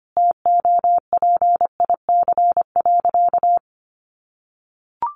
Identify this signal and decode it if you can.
Morse code, heard as a keyed tone, about 25 wpm, 700 Hz. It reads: TOPIC.
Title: Sentences from Top 1200 Words 25wpm